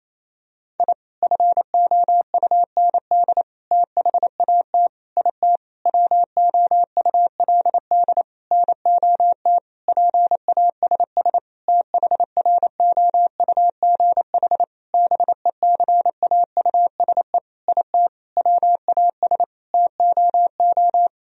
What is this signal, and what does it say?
Morse code, heard as a keyed tone, about 28 wpm, 700 Hz. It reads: I FOUND THAT IT WOULD NOT PASS THROUGH BECAUSE IT WAS TOO
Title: Sentences from Top 1000 Words - Review Part 1 - 28wpm